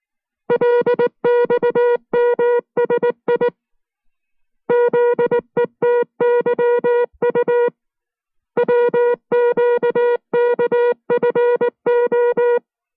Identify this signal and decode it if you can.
Morse code, heard as a keyed tone, about 19 wpm, 480 Hz. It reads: LXMSI ZETYU WQKFO